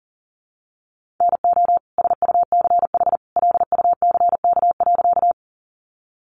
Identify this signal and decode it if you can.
Morse code, heard as a keyed tone, about 40 wpm, 700 Hz. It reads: NO SUCH LUCK.
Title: Sentences from Top 1500 Words 40wpm